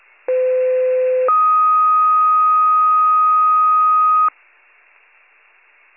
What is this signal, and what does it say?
Two-tone page, heard as a one-second tone, then a three-second tone, about 510 then 1250 Hz. It is heard from 0.3 to 4.3 s.